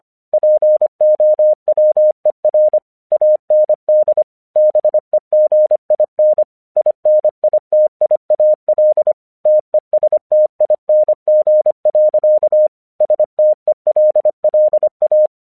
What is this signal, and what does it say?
Morse code, heard as a keyed tone, about 25 wpm, 610 Hz. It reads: POWER AND BEGIN INITIAL TESTING. STELLA
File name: 1 English 25wpm / AQC27